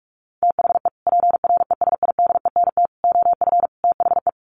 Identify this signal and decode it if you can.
Morse code, heard as a keyed tone, about 44 wpm, 700 Hz. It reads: THE PRESIDENT OF THE
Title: Sets of 4 Words 45wpm